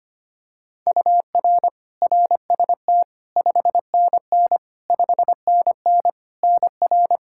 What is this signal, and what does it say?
Morse code, heard as a keyed tone, about 25 wpm, 700 Hz. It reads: UR RST 5NN 5NN NR
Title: K1USN SST QSO Practice 25wpm